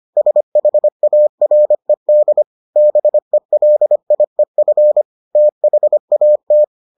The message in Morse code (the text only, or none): SHARED BELIEF THAT